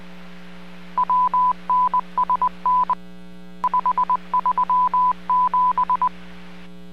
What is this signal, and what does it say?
Morse code, heard as a keyed tone, about 20 wpm, 1000 Hz. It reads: WNSN 537